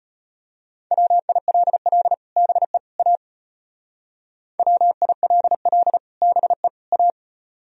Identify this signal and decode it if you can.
Morse code, heard as a keyed tone, about 32 wpm, 700 Hz. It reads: WILL BE A WILL BE A